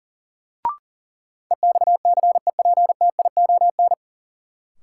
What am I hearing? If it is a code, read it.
Morse code, heard as a keyed tone, about 40 wpm, 700 Hz. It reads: EXCEPTION
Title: Sets of 4 Words - New Word Drill - 40wpm